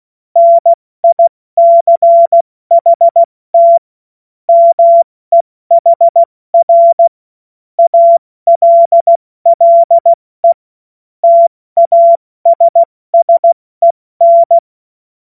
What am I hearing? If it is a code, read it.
Morse code, heard as a keyed tone, about 16 wpm, 680 Hz. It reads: NICHT MEHR ALLE TASSEN
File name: Morsekurs / Witze / Witz 07. September 2021, 70 F80